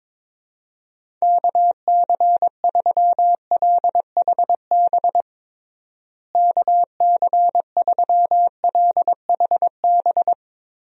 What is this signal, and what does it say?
Morse code, heard as a keyed tone, about 22 wpm, 700 Hz. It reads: KC3LHB KC3LHB
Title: USA Call Signs - 6 Characters - Repeated 3x - 22wpm